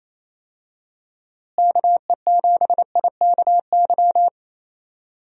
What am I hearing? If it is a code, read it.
Morse code, heard as a keyed tone, about 27 wpm, 700 Hz. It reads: KE7IKY
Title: USA Call Signs - 6 Characters - Repeated 2x - 28wpm